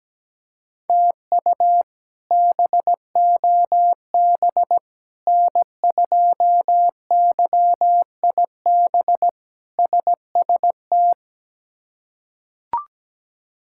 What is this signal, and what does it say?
Morse code, heard as a keyed tone, about 17 wpm, 700 Hz. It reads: TU BOB N2YIB SST
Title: K1USN SST QSO Practice - 17wpm